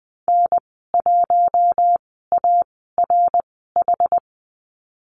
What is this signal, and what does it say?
Morse code, heard as a keyed tone, about 20 wpm, spaced as for 14 wpm, 700 Hz. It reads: N1ARH
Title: 122 Lesson 20g - Number 3 - Call Signs 20wpm